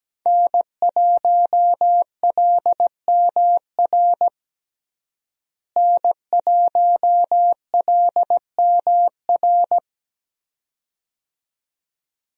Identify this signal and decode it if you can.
Morse code, heard as a keyed tone, about 17 wpm, 700 Hz. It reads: N1LMR N1LMR
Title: USA Call Signs - 5 Characters - Repeated 2x - 17wpm